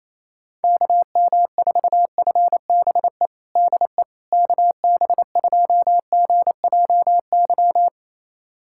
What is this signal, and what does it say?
Morse code, heard as a keyed tone, about 27 wpm, 700 Hz. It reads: KM4FBE DE KB2GJY
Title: USA Callsign DE Callsign - 6 Characters - 28wpm